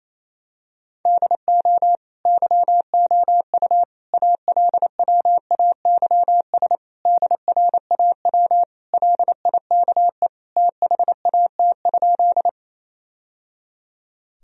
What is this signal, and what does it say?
Morse code, heard as a keyed tone, about 27 wpm, 700 Hz. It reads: DO YOU ALWAYS DRAW LIKE THAT?
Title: Sentences from Top 1000 Words - Review Part 1 - 28wpm